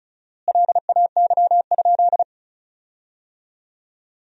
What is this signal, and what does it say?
Morse code, heard as a keyed tone, about 36 wpm, 700 Hz. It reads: LAY?
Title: 4x Word-Spacing - Sentences from Top 400 Words - 35wpm